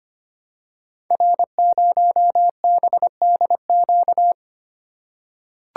Morse code, heard as a keyed tone, about 25 wpm, 700 Hz. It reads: R0BDQ
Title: Code Groups - 5 Letter:Numbers - 25wpm